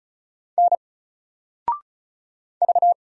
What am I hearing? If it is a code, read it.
Morse code, heard as a keyed tone, about 34 wpm, 700 Hz. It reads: N V
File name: Single-Letter-Number-35wpm